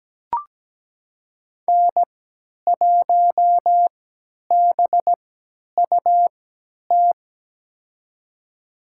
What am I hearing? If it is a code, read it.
Morse code, heard as a keyed tone, about 17 wpm, spaced as for 10 wpm, 700 Hz. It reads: N1BUT
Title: USA Call Signs - 3x Character Spacing - 17wpm